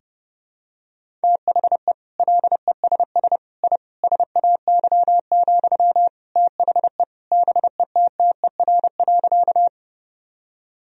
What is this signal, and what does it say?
Morse code, heard as a keyed tone, about 30 wpm, 700 Hz. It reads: THE LESS I SAY, THE BETTER.
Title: Sentences from Top 1000 Words - Review Part 2 - 30wpm